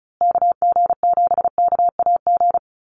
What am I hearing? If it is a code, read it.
Morse code, heard as a keyed tone, about 34 wpm, 700 Hz. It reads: KG7KAG
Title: USA Call Signs Repeated 3x 35wpm